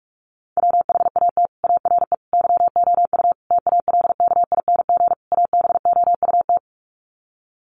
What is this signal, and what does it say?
Morse code, heard as a keyed tone, about 44 wpm, 700 Hz. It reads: WHAT ARE YOU TALKING ABOUT
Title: Sets of 5 Words 45wpm